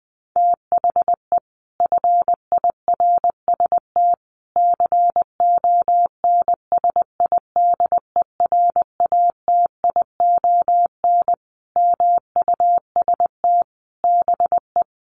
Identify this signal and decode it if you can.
Morse code, heard as a keyed tone, about 20 wpm, 700 Hz. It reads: THE FIRST CONSIDERATION MUST BE
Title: Sentences from Top 1500 Words 20wpm